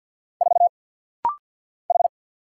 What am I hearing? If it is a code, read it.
Morse code, heard as a keyed tone, about 49 wpm, 700 Hz. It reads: 4 H
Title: Single Letter-Number Alt - Rapid-Fire 50wpm